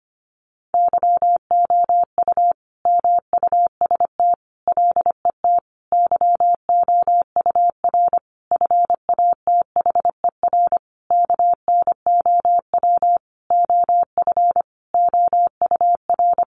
Morse code, heard as a keyed tone, about 25 wpm, 700 Hz. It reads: YOU MUST LET YOUR FATHER KNOW OF OUR
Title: Sentences from Top 1500 Words 25wpm